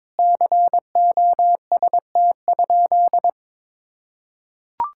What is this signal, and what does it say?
Morse code, heard as a keyed tone, about 21 wpm, 700 Hz. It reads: COST?
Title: Sentences from Top 1000 Words - Review Part 6 - 22wpm